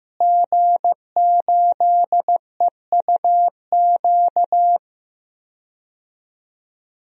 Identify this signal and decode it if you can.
Morse code, heard as a keyed tone, about 15 wpm, 700 Hz. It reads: G8EUQ
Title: DXCC-100 Call Signs - 5 Characters - Repeated 3x - 15wpm